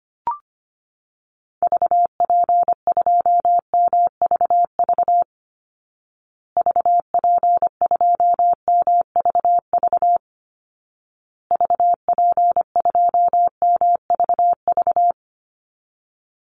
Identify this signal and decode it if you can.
Morse code, heard as a keyed tone, about 25 wpm, 700 Hz. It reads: VP2MVV VP2MVV VP2MVV
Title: Call Signs Repeated 2x - 25wpm